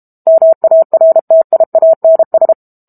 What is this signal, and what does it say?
Morse code, heard as a keyed tone, about 32 wpm, 650 Hz. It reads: MARTIANS
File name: Book Two / 06-Chapter 6 - 32 WPM